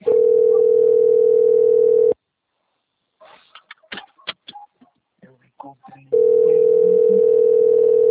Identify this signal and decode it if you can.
Ringback tone (North America), 440 + 480 Hz, with speech.
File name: out- - - - - .22448